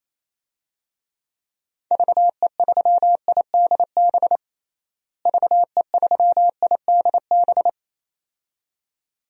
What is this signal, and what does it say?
Morse code, heard as a keyed tone, about 28 wpm, 700 Hz. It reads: VE3IDB VE3IDB
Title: DXCC-100 Call Signs - 6 Characters - Repeated 2x - 28wpm